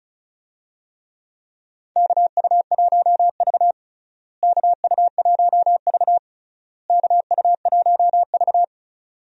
Morse code, heard as a keyed tone, about 34 wpm, 700 Hz. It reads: KU1V KU1V KU1V
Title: USA Call Signs - 4 Characters - Repeated 3x - 35wpm